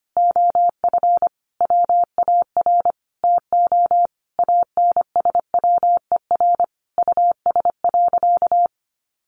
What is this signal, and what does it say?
Morse code, heard as a keyed tone, about 25 wpm, 700 Hz. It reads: OF WAR TO ANSWER US.
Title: Sentences from Top 1500 Words 25wpm